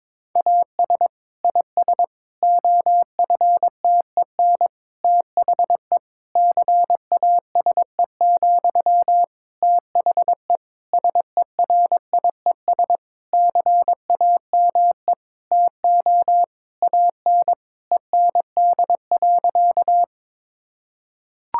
Morse code, heard as a keyed tone, about 22 wpm, 700 Hz. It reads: AS IS OFTEN THE CASE, THE SERIES CAME TO AN END.
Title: Sentences from Top 1000 Words - Review Part 5 - 22wpm